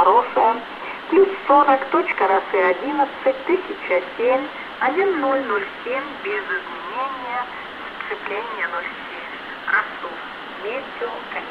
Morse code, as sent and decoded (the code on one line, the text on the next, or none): none